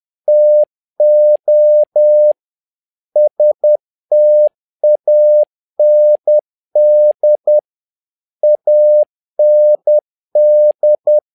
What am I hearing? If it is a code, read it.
Morse code, heard as a keyed tone, about 10 wpm, 600 Hz. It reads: TO STAND AND
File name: learning center / tower trouble / tower trouble 10wpm